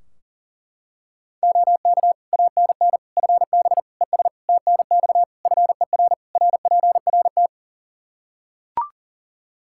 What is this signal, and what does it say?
Morse code, heard as a keyed tone, about 40 wpm, 700 Hz. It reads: OK ANN FB ES TNX FER RPRT
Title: K1USN SST QSO Practice 40wpm